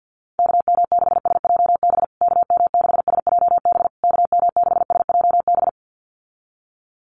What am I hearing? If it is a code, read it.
Morse code, heard as a keyed tone, about 49 wpm, 700 Hz. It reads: KM6SJB KM6SJB KM6SJB
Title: USA Call Signs Repeated 3x 50wpm